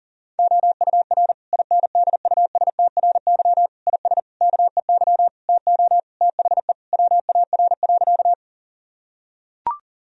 Morse code, heard as a keyed tone, about 37 wpm, 700 Hz. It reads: OUR INDUSTRY IS KEY TO THE WAR.
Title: Sentences from Top 1000 Words - Review Part 4 - 40wpm